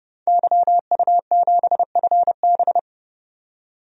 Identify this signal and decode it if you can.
Morse code, heard as a keyed tone, about 30 wpm, 700 Hz. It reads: YU7FB
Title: DXCC-100 Call Signs - 5 Characters - 30wpm